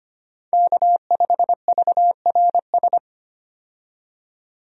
Morse code, heard as a keyed tone, about 25 wpm, 700 Hz. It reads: K5VRS
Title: USA Call Signs - 5 Characters - 25wpm